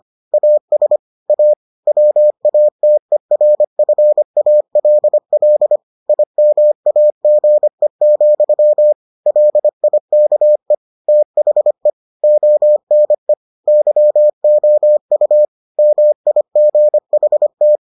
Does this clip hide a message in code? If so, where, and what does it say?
Yes: Morse code from 0.3 to 17.8 s, heard as a keyed tone, about 25 wpm, 600 Hz. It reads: AS A WATERFALL IMAGE, LIKE THE ONE YOU MIGHT